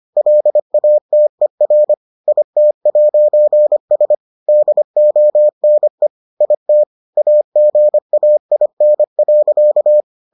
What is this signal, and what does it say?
Morse code, heard as a keyed tone, about 25 wpm, 600 Hz. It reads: LATER IT'S DONE IT AGAIN.